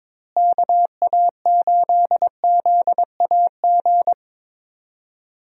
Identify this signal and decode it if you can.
Morse code, heard as a keyed tone, about 22 wpm, 700 Hz. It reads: KA8ZAG
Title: USA Call Signs - 6 Characters - Repeated 2x - 22wpm